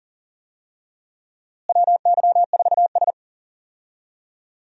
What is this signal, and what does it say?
Morse code, heard as a keyed tone, about 37 wpm, 700 Hz. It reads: WY4S
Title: USA Call Signs - 4 Characters - Repeated 2x - 40wpm